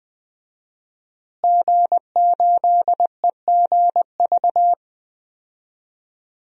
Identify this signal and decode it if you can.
Morse code, heard as a keyed tone, about 20 wpm, 700 Hz. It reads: G8EGV
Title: DXCC-100 Call Signs - 5 Characters - Repeated 2x - 20wpm